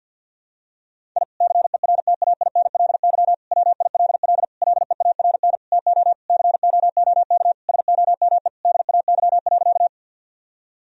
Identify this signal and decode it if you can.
Morse code, heard as a keyed tone, about 50 wpm, 700 Hz. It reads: I CERTAINLY WILL LEARN TO COOK SOME DAY.
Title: Sentences from Top 1000 Words - Review Part 4 - 50wpm